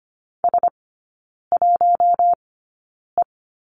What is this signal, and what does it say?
Morse code, heard as a keyed tone, about 25 wpm, spaced as for 9 wpm, 700 Hz. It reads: S1E
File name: course / 25:10wpm / 093 Lesson 16e - Letter c - Three Characters 25wpm